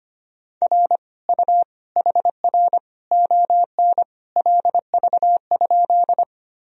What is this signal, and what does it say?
Morse code, heard as a keyed tone, about 25 wpm, 700 Hz. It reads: R U HR ON LV?
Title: Sentences from Top 400 Words and Abbreviations 25wpm